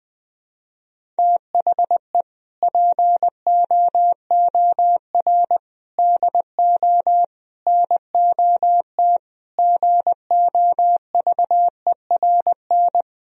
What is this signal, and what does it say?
Morse code, heard as a keyed tone, about 21 wpm, 700 Hz. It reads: THE POOR DO NOT GOVERN
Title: Sentences from Top 1000 Words - Review Part 2 - 20wpm